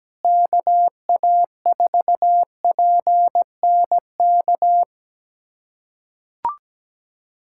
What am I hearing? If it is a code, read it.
Morse code, heard as a keyed tone, about 17 wpm, 700 Hz. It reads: KA4PNK
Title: USA Callsign DE Callsign 17wpm